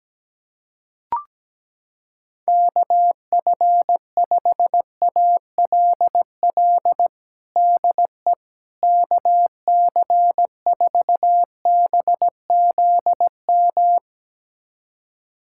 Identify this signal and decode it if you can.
Morse code, heard as a keyed tone, about 17 wpm, 700 Hz. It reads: KF5ALL DE KC4BZM
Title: USA Callsign DE Callsign - 6 Characters - 17wpm